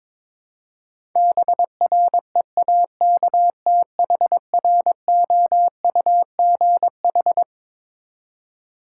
Morse code, heard as a keyed tone, about 22 wpm, 700 Hz. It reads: BREAKTHROUGH
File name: Twelve Letter Words - 22wpm